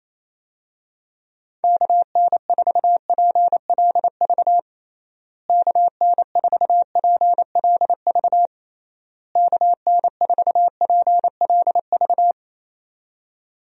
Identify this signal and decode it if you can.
Morse code, heard as a keyed tone, about 28 wpm, 700 Hz. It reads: KN4PLV KN4PLV KN4PLV
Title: USA Call Signs - 6 Characters - Repeated 3x - 28wpm